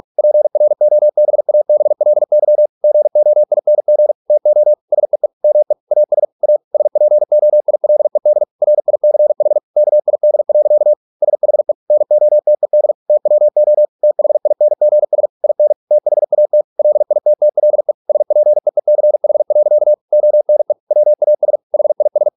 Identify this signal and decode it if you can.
Morse code, heard as a keyed tone, about 46 wpm, 600 Hz. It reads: PROBABLY GOING TO SEE ME AS A SPOILED RICH KID. SHE NOTED TWO THINGS IN THAT LITTLE SPEECH. ONE WAS HIS